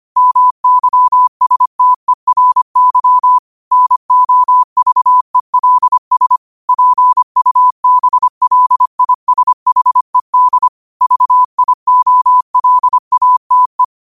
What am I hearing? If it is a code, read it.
Morse code, heard as a keyed tone, about 25 wpm, 1000 Hz. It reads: MYSTERY NOVELS PUBLISHED VIOLATE
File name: qotdfiles / 25wpm